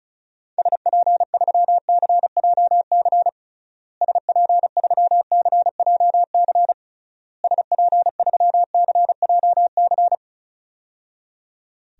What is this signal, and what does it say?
Morse code, heard as a keyed tone, about 36 wpm, 700 Hz. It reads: SP3CJC SP3CJC SP3CJC
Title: DXCC-100 Call Signs - 6 Characters - Repeated 3x - 35wpm